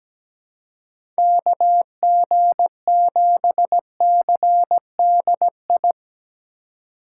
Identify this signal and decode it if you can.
Morse code, heard as a keyed tone, about 17 wpm, 700 Hz. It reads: KG7CDI
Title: USA Call Signs - 6 Characters - 3x Character Spacing - 17wpm